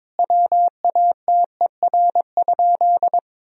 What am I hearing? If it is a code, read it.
Morse code, heard as a keyed tone, about 21 wpm, 700 Hz. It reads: WATER?
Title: Sentences from Top 1000 Words - Review Part 3 - 22wpm